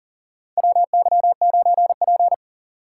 Morse code, heard as a keyed tone, about 40 wpm, 700 Hz. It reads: WY9P
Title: USA Call Signs - 4 Characters - 2x Character Spacing - 40wpm